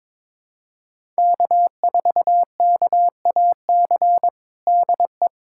Morse code, heard as a keyed tone, about 22 wpm, 700 Hz. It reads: K4KAC DE
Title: USA Callsign DE Callsign 22wpm